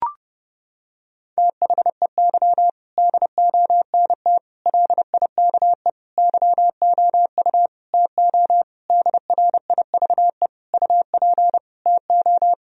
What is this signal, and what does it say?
Morse code, heard as a keyed tone, about 30 wpm, 700 Hz. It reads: THEY DONT LIKE YOU TO DRIVE UP TO
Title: Sentences from Top 1000 Words - Review Part 2 - 30wpm